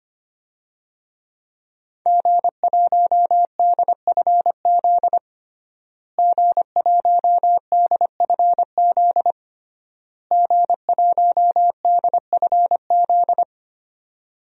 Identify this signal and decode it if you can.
Morse code, heard as a keyed tone, about 25 wpm, 700 Hz. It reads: G1DFZ G1DFZ G1DFZ
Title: DXCC-100 Call Signs - 5 Characters - Repeated 3x - 25wpm